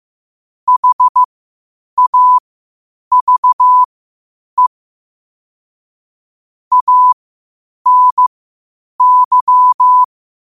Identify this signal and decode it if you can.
Morse code, heard as a keyed tone, about 15 wpm, spaced as for 8 wpm, 1000 Hz. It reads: HAVE ANY
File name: qotdfiles / 8wpm